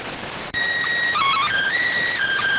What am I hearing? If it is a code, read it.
SSTV (slow-scan television), PD 290.